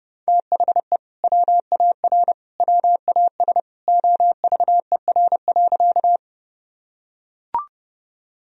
Morse code, heard as a keyed tone, about 30 wpm, 700 Hz. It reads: THE WAR WAS OVER.
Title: Sentences from Top 1000 Words - Review Part 2 - 30wpm